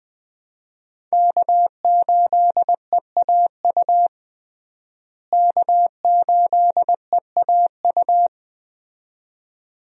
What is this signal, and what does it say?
Morse code, heard as a keyed tone, about 20 wpm, 700 Hz. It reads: K8EAU K8EAU
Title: USA Call Signs - 5 Characters - Repeated 2x - 20wpm